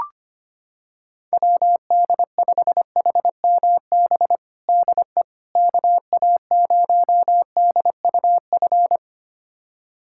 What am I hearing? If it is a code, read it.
Morse code, heard as a keyed tone, about 25 wpm, 700 Hz. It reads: WD5HMB DE KA0DUF
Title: USA Callsign DE Callsign 25wpm